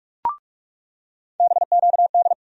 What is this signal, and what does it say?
Morse code, heard as a keyed tone, about 46 wpm, 700 Hz. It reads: BQD
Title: USA Call Sign Postfixes - 3 Letters 45wpm